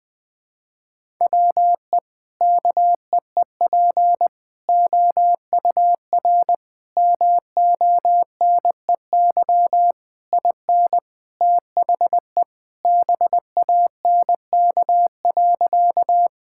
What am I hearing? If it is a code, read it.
Morse code, heard as a keyed tone, about 20 wpm, 700 Hz. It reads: WE KEEP OUR MONEY IN THE BANK.
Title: Sentences from Top 1000 Words - Review Part 6 - 20wpm